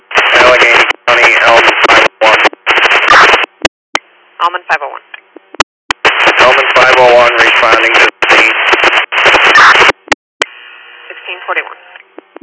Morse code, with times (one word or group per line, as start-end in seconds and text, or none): none